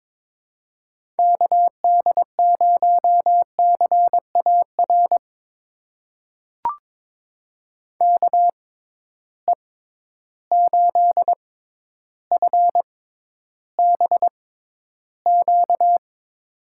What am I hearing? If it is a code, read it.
Morse code, heard as a keyed tone, about 22 wpm, spaced as for 11 wpm, 700 Hz. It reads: KD0CAR KE8FBQ
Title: USA Call Signs - 6 Characters - 6x Character Spacing - 22wpm